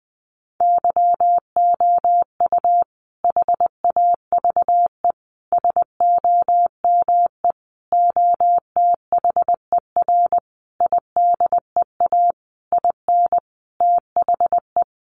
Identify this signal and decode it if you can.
Morse code, heard as a keyed tone, about 20 wpm, 700 Hz. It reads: YOU HAVE SOME OTHER IDEA IN THE
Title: Sentences from Top 1300 Words 20wpm